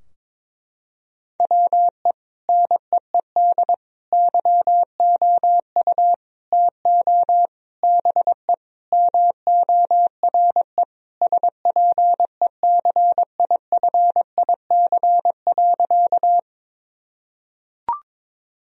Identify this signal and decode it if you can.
Morse code, heard as a keyed tone, about 22 wpm, 700 Hz. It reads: WE NEED YOU TO BE MORE SPECIFIC.
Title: Sentences from Top 1000 Words - Review Part 3 - 22wpm